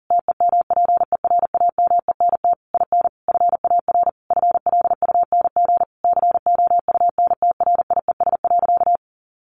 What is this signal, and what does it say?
Morse code, heard as a keyed tone, about 40 wpm, 700 Hz. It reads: TEMPERAMENT IN FAR FLUNG COUNTRIES.
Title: A Sound Of Thunder 40wpm